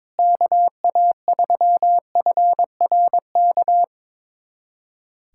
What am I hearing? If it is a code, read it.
Morse code, heard as a keyed tone, about 22 wpm, 700 Hz. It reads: KA3FRK
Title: USA Callsign DE Callsign - 6 Characters - 22wpm